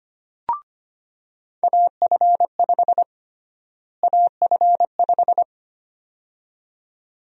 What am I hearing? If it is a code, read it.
Morse code, heard as a keyed tone, about 25 wpm, 700 Hz. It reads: AF5 AF5
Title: USA Call Sign Prefixes - Repeated 2x - 25wpm